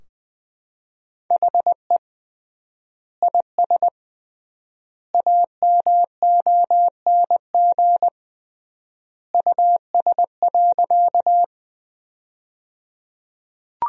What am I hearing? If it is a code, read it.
Morse code, heard as a keyed tone, about 21 wpm, 700 Hz. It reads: HE IS AMONG US.
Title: 3x Word-Spacing - Sentences from Top 500 Words - 20wpm